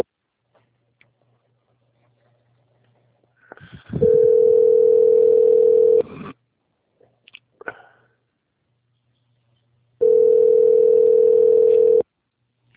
Ringback tone (North America), 440 + 480 Hz.